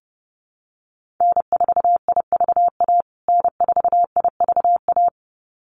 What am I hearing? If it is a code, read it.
Morse code, heard as a keyed tone, about 30 wpm, 700 Hz. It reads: N4IVA N4IVA
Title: USA Call Signs Repeated 2x 30wpm